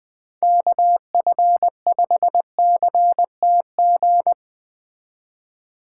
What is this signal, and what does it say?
Morse code, heard as a keyed tone, about 21 wpm, 700 Hz. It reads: KF5CTG
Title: Speed Racing - USA 2x3 Call Signs 17wpm